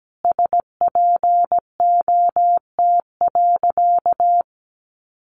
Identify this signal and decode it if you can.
Morse code, heard as a keyed tone, about 17 wpm, 700 Hz. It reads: SPOT.
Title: Sentences from Top 1100 Words 17wpm